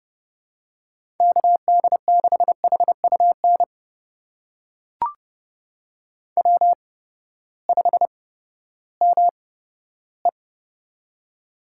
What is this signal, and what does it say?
Morse code, heard as a keyed tone, about 29 wpm, spaced as for 15 wpm, 700 Hz. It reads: KD6HUN W5ME
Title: USA Call Signs - 8x Character Spacing - 30wpm